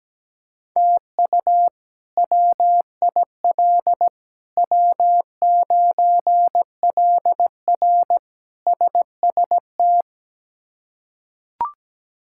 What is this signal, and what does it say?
Morse code, heard as a keyed tone, about 17 wpm, 700 Hz. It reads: TU WIL W9LR SST
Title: K1USN SST QSO Practice - 17wpm